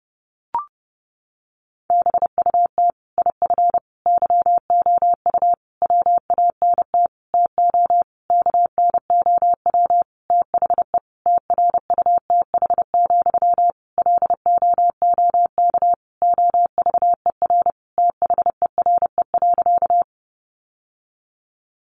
Morse code, heard as a keyed tone, about 30 wpm, 700 Hz. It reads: BUT IF YOU WANT TO KNOW THE TRUTH, LOOK OVER THERE.
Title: Sentences from Top 800 Words 30wpm